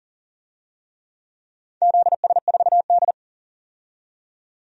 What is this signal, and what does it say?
Morse code, heard as a keyed tone, about 42 wpm, 700 Hz. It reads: ZS4D
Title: DXCC-100 Call Signs - 4 Characters - 40wpm